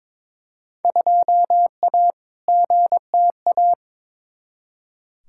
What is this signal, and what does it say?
Morse code, heard as a keyed tone, about 21 wpm, 700 Hz. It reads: 2A GTA
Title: Field Day Exchanges Encore - 22wpm